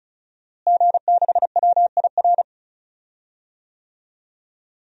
Morse code, heard as a keyed tone, about 34 wpm, 700 Hz. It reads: G6WIR
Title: DXCC-100 Call Signs - 5 Characters - Repeated 3x - 35wpm